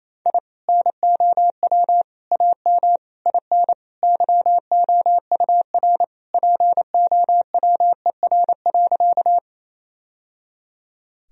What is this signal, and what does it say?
Morse code, heard as a keyed tone, about 27 wpm, 700 Hz. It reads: I NOW AM IN YOUR POWER.
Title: Sentences from Top 1000 Words - Review Part 2 - 28wpm